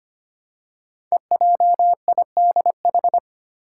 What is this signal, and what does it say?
Morse code, heard as a keyed tone, about 25 wpm, 700 Hz. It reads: EJIDH
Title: Code Groups - 5 Letters - 25wpm